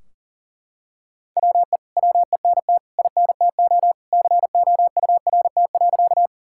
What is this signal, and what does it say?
Morse code, heard as a keyed tone, about 37 wpm, 700 Hz. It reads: WE WENT INTO COURT.
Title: Sentences from Top 1600 Words 40wpm